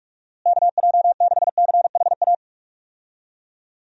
Morse code, heard as a keyed tone, about 43 wpm, 700 Hz. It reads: KJ6CHA
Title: USA Call Signs - 6 Characters - Repeated 2x - 45wpm